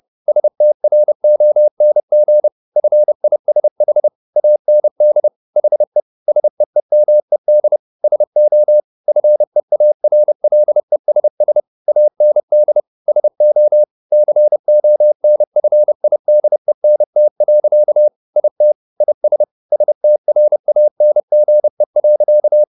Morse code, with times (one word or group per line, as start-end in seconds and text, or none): 0.3-2.5 STRONG
2.8-4.1 FISH
4.4-5.3 AND
5.6-6.0 HE
6.3-7.8 SEEMED
8.0-8.8 SO
9.1-11.6 FEARLESS
11.9-12.8 AND
13.1-13.8 SO
14.1-18.1 CONFIDENT.
18.4-18.7 IT
19.0-19.4 IS
19.7-22.6 STRANGE.